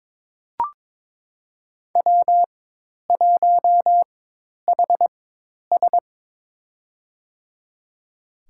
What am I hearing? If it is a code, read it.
Morse code, heard as a keyed tone, about 22 wpm, spaced as for 10 wpm, 700 Hz. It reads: W1HS
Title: USA Call Signs - 4 Characters - 4x Character Spacing - 22wpm